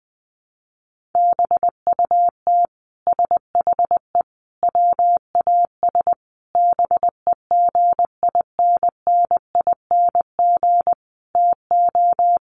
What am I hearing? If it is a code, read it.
Morse code, heard as a keyed tone, about 20 wpm, 700 Hz. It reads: BUT SHE WAS BEGINNING TO
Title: Sentences from Top 1500 Words 20wpm